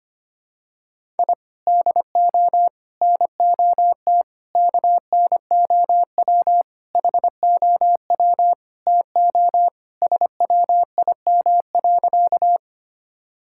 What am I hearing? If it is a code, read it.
Morse code, heard as a keyed tone, about 25 wpm, 700 Hz. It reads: I DO NOT KNOW HOW TO SWIM.
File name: practice / 25wpm / Sentences with 4-Word Phrases - 25wpm